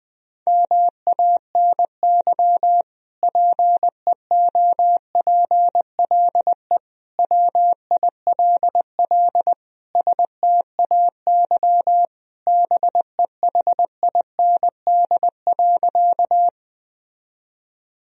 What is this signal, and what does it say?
Morse code, heard as a keyed tone, about 21 wpm, 700 Hz. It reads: MANY PEOPLE WILL STAY BEHIND.
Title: Sentences from Top 1000 Words - Review Part 1 - 20wpm